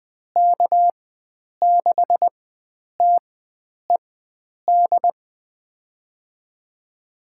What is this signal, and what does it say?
Morse code, heard as a keyed tone, about 20 wpm, spaced as for 9 wpm, 700 Hz. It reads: K6TED
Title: USA Call Signs - 4x Character Spacing - 20wpm